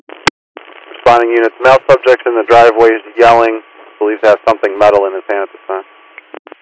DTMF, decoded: none

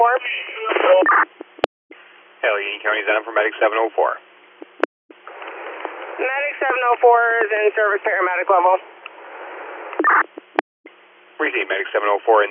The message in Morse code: none

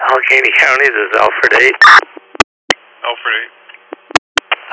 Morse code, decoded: none

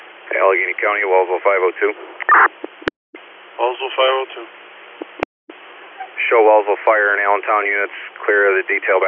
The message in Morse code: none

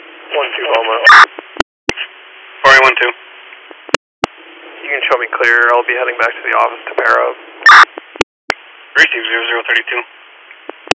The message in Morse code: none

